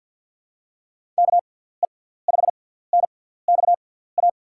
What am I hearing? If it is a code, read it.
Morse code, heard as a keyed tone, about 47 wpm, spaced as for 17 wpm, 700 Hz. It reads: KE5NXA
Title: USA Call Signs - 6 Characters - 6x Character Spacing - 50wpm